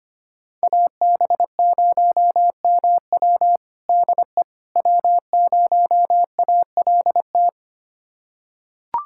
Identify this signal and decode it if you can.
Morse code, heard as a keyed tone, about 25 wpm, 700 Hz. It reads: AB0MW DE W0ALT
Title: USA Callsign DE Callsign - 5 Characters - 25wpm